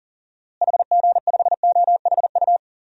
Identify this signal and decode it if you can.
Morse code, heard as a keyed tone, about 37 wpm, 700 Hz. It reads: HG5OHU
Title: DXCC-100 Call Signs - 6 Characters - Repeated 3x - 40wpm